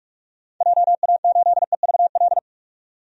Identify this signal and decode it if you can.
Morse code, heard as a keyed tone, about 46 wpm, 700 Hz. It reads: JA8EVL
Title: DXCC-100 Call Signs - 6 Characters - 45wpm